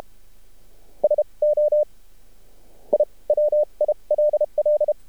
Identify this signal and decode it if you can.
Morse code, heard as a keyed tone, about 32 wpm, spaced as for 26 wpm, 600 Hz. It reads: SO I WILL
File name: archived / CRSnet